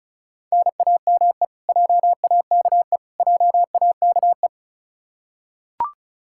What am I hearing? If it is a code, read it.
Morse code, heard as a keyed tone, about 36 wpm, 700 Hz. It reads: NAME JAKE JAKE
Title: Practice QSO - Exchange 1 - Beginning - 35wpm